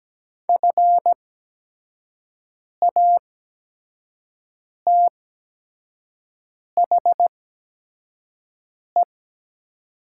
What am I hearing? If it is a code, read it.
Morse code, heard as a keyed tone, about 17 wpm, spaced as for 5 wpm, 700 Hz. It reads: FATHE
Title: 8x Character Spacing - Top 200 Words - 17wpm